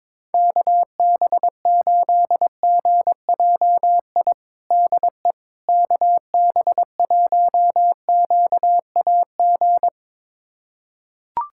Morse code, heard as a keyed tone, about 22 wpm, 700 Hz. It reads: KB8GJI DE KB1QAG
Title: USA Callsign DE Callsign 22wpm